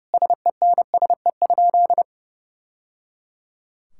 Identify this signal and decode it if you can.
Morse code, heard as a keyed tone, about 30 wpm, 700 Hz. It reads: SENSE?
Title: Sentences from Top 1000 Words - Review Part 5 - 30wpm